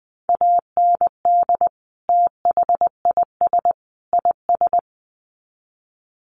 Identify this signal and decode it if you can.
Morse code, heard as a keyed tone, about 20 wpm, 700 Hz. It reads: AND THIS IS
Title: Sets of 3 Words 20wpm